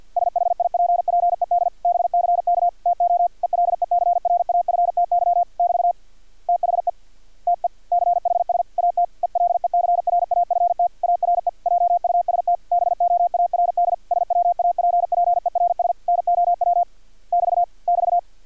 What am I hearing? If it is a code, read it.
Morse code, heard as a keyed tone, about 50 wpm, 700 Hz. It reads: SHIPPED BCK TO ELECRAFTY <BT> THE TECHS AT ELECRAFT ARE JUST BOARD SWAPPERS NOW <BT> <BT>
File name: archived / CRSnet